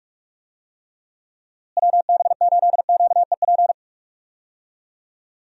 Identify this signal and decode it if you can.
Morse code, heard as a keyed tone, about 46 wpm, 700 Hz. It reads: WB8QEP
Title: USA Call Signs - 6 Characters - Repeated 2x - 45wpm